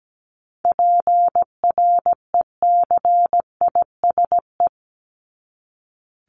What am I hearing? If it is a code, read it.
Morse code, heard as a keyed tone, about 17 wpm, 700 Hz. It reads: PRECISE
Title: LY Suffix - Top 100 Words with Base Words 17wpm